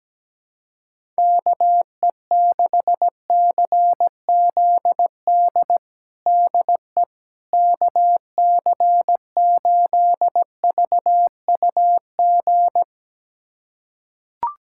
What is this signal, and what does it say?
Morse code, heard as a keyed tone, about 17 wpm, 700 Hz. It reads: KE6CZD DE KC8VUG
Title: USA Callsign DE Callsign - 6 Characters - 17wpm